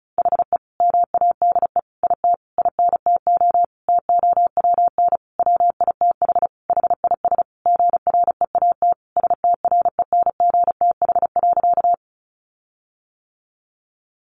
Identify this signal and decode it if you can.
Morse code, heard as a keyed tone, about 34 wpm, 700 Hz. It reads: HE MADE IT INTO TOWN WITH HIS GREAT STRENGTH.
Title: Sentences from Top 1300 Words 35wpm